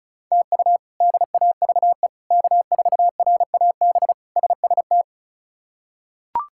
Morse code, heard as a keyed tone, about 36 wpm, 700 Hz. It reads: TU DAVE K4RAB SST
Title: K1USN SST QSO Practice - 35wpm